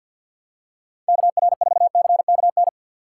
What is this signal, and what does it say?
Morse code, heard as a keyed tone, about 46 wpm, 690 Hz. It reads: KR4CKN
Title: USA Call Signs - 2x Character Spacing - 50wpm